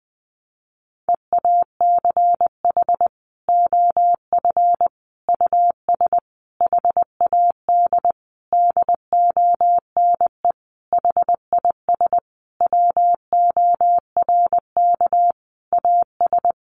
Morse code, heard as a keyed tone, about 20 wpm, 700 Hz. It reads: EACH OF US HAD DONE HIS WORK AS
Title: Sentences from Top 500 Words 20wpm